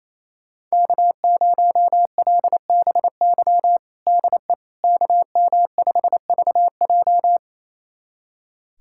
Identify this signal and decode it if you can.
Morse code, heard as a keyed tone, about 28 wpm, 700 Hz. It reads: K0LBY DE KM5VJ
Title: USA Callsign DE Callsign - 5 Characters - 28wpm